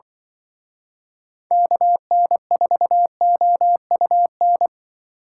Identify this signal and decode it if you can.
Morse code, heard as a keyed tone, about 24 wpm, 700 Hz. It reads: KN4OUN
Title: Speed Racing - USA 2x3 Call Signs 20wpm